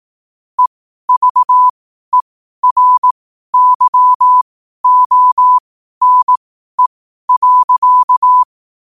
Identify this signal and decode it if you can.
Morse code, heard as a keyed tone, about 18 wpm, spaced as for 12 wpm, 1000 Hz. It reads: EVERYONE.
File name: qotdfiles / 12wpm